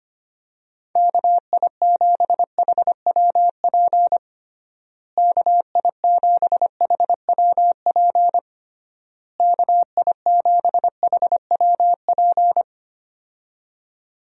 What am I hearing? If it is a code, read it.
Morse code, heard as a keyed tone, about 25 wpm, 700 Hz. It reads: KI7HWP KI7HWP KI7HWP
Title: USA Call Signs - 6 Characters - Repeated 3x - 25wpm